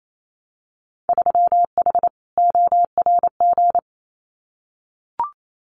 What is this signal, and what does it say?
Morse code, heard as a keyed tone, about 28 wpm, 700 Hz. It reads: 3H ORG
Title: Winter Field Day Exchanges Encore 28wpm